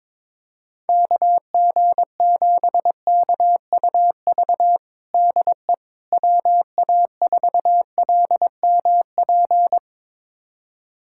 Morse code, heard as a keyed tone, about 22 wpm, 700 Hz. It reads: KG7KUV DE WA4LMP
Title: USA Callsign DE Callsign - 6 Characters - 22wpm